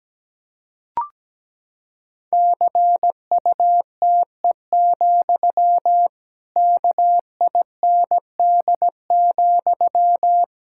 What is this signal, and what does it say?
Morse code, heard as a keyed tone, about 17 wpm, 700 Hz. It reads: CUTE, KIND,
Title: Pangram Challenge v2 - 17wpm